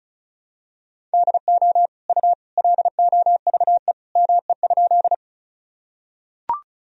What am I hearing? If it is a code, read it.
Morse code, heard as a keyed tone, about 36 wpm, 700 Hz. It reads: DO U LOVE ME?
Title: Sentences from Top 400 Words and Abbreviations 35wpm